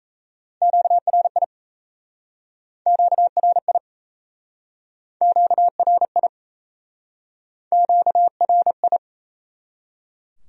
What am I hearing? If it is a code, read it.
Morse code, heard as a keyed tone, about 38 wpm, 700 Hz. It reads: QRI QRI QRI QRI